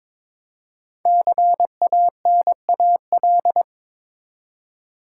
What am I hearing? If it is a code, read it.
Morse code, heard as a keyed tone, about 22 wpm, 690 Hz. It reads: CANAL
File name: 5 Letter Words - First 3 Letters Given - 22wpm